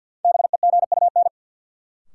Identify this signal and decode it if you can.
Morse code, heard as a keyed tone, about 53 wpm, 700 Hz. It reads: BEGUN
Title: Sentences from Top 1000 Words - Review Part 6 - New Word Drill - 50wpm